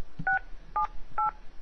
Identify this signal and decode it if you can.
DTMF, heard as telephone touch tones, tones of about 130 ms. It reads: 678